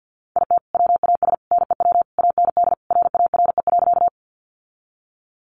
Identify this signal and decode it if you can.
Morse code, heard as a keyed tone, about 50 wpm, 700 Hz. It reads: IT WAS NEW AND RARE.
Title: Sentences from Top 1100 Words 50wpm